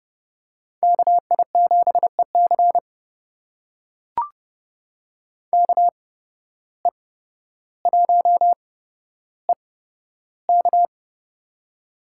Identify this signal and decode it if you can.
Morse code, heard as a keyed tone, about 31 wpm, spaced as for 13 wpm, 700 Hz. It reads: KI7EC KE1EK
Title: USA Call Signs - 5 Characters - 8x Character Spacing - 30wpm